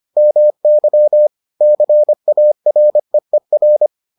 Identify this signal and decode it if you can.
Morse code, heard as a keyed tone, about 25 wpm, 600 Hz. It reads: MY CAREER